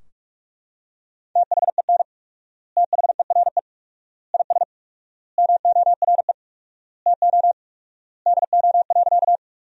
Morse code, heard as a keyed tone, about 43 wpm, 700 Hz. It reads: THEN THERE IS MORE TO DO.
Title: 4x Word-Spacing - Sentences from Top 100 Words - 45wpm